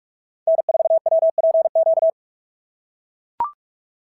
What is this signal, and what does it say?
Morse code, heard as a keyed tone, about 44 wpm, 690 Hz. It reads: N4WPQ
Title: SOTA QSO Practice 45wpm